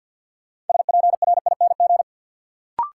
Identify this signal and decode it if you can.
Morse code, heard as a keyed tone, about 53 wpm, 700 Hz. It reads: SPRING
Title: Sentences from Top 1200 Words - New Word Drill Alt - 50wpm